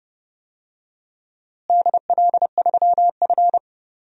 Morse code, heard as a keyed tone, about 31 wpm, 700 Hz. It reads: DL3F
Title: DXCC-100 Call Signs - 4 Characters - Repeated 3x - 30wpm